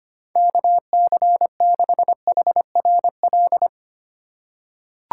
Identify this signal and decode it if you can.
Morse code, heard as a keyed tone, about 25 wpm, 700 Hz. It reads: KC6HRL
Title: USA Call Signs - 6 Characters - 4x Character Spacing - 25wpm